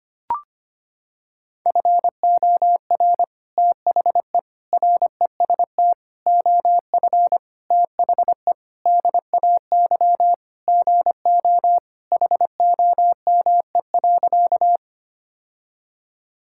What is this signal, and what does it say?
Morse code, heard as a keyed tone, about 25 wpm, 700 Hz. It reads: FOR THE REST OF THE DAY GO HOME.
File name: practice / 25wpm / Sentences with 3-Word Phrases - 25wpm